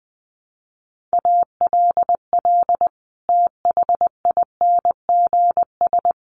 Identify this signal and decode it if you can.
Morse code, heard as a keyed tone, about 20 wpm, 700 Hz. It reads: ALL THINGS
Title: Sets of 5 Words Encore 20wpm